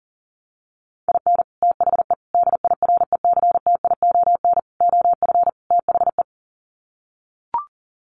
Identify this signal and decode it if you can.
Morse code, heard as a keyed tone, about 40 wpm, 700 Hz. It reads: IN THE DIRECTION OF THE
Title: Sets of 5 Words 40wpm